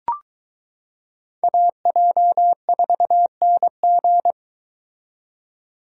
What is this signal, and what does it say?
Morse code, heard as a keyed tone, about 23 wpm, 700 Hz. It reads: AJ4NG